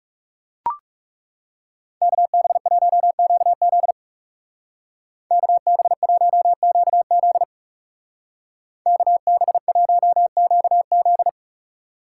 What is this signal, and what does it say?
Morse code, heard as a keyed tone, about 40 wpm, 700 Hz. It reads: KB1QZ KB1QZ KB1QZ